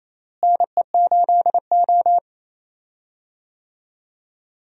Morse code, heard as a keyed tone, about 28 wpm, 700 Hz. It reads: NE8O
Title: USA Call Signs - 4 Characters - Repeated 2x - 28wpm